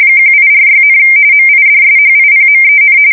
RTTY, about 45.45 baud. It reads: 1846, MIDSHIPMEN
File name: fsk170 / usnavy2